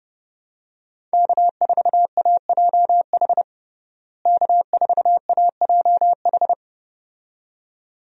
Morse code, heard as a keyed tone, about 30 wpm, 700 Hz. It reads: K4AJH K4AJH
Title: USA Call Signs - 5 Characters - Repeated 2x - 30wpm